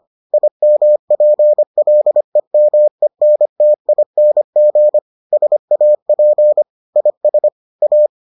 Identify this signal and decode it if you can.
Morse code, heard as a keyed tone, about 25 wpm, 600 Hz. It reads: IMPLEMENTING SAP IS A